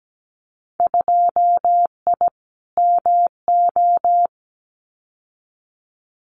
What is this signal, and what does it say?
Morse code, heard as a keyed tone, about 17 wpm, 700 Hz. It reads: 2I MO
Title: Winter Field Day Exchanges 17wpm